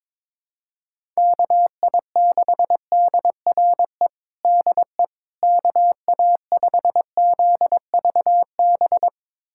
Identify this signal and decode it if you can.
Morse code, heard as a keyed tone, about 22 wpm, 700 Hz. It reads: KI6DRE DE KA5ZVB
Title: USA Callsign DE Callsign 22wpm